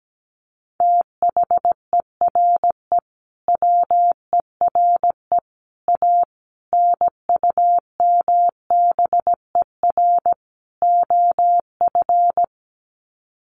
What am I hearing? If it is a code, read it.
Morse code, heard as a keyed tone, about 17 wpm, 700 Hz. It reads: THERE WERE A NUMBER OF
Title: Sets of 5 Words Encore2 17wpm